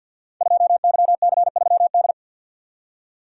Morse code, heard as a keyed tone, about 50 wpm, 690 Hz. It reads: 2YC3D
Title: Code Groups - 5 Letter:Numbers - 50wpm